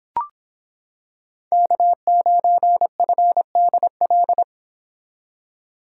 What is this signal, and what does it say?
Morse code, heard as a keyed tone, about 25 wpm, 700 Hz. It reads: K9FDL